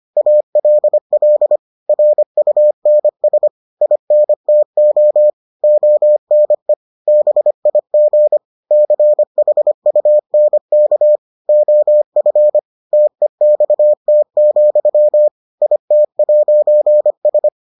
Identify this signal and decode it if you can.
Morse code, heard as a keyed tone, about 25 wpm, 600 Hz. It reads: ALL RUNS INTO ONE BIG CHUNK OF TEXT, IT'S